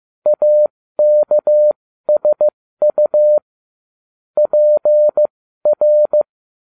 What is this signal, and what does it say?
Morse code, heard as a keyed tone, about 15 wpm, spaced as for 13 wpm, 600 Hz. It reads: AKSU PR